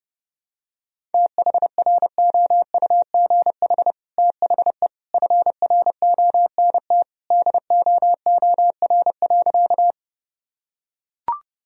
Morse code, heard as a keyed tone, about 32 wpm, 700 Hz. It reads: THROUGH THE FRONT DOOR.
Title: Sentences from Top 1000 Words - Review Part 2 - 30wpm